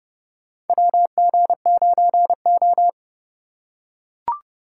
Morse code, heard as a keyed tone, about 30 wpm, 700 Hz. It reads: WG9O
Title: USA Call Signs - 3x Character Spacing - 30wpm